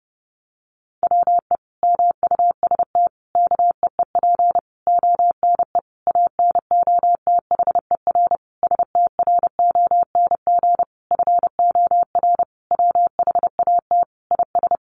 Morse code, heard as a keyed tone, about 28 wpm, 700 Hz. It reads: WE MUST KEEP ONE ANOTHER STRONG FOR WHAT IS
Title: Sentences from Top 600 Words 30wpm